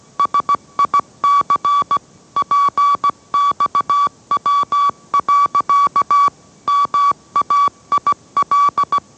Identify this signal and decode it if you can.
Morse code, heard as a keyed tone, about 17 wpm, 1140 Hz. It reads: SIC PXW. MAIL
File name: mrx / mrx080 18wpm